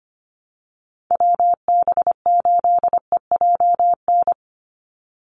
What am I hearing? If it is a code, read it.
Morse code, heard as a keyed tone, about 25 wpm, 700 Hz. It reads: WB8EJN
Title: Call Signs - 25wpm